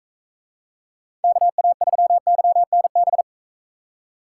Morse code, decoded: KA3YNB

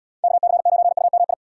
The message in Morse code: KC9UMI

